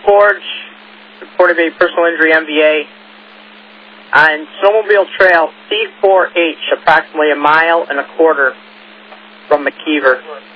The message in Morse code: none

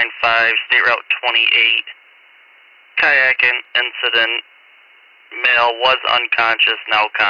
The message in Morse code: none

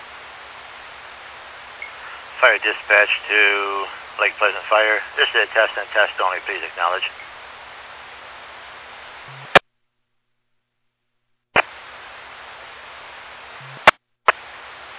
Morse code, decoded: none

